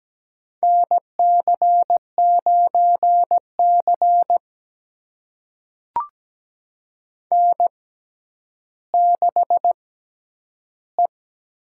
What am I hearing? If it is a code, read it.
Morse code, heard as a keyed tone, about 17 wpm, spaced as for 10 wpm, 700 Hz. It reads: NC9C N6E
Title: USA Call Signs - 6x Character Spacing - 17wpm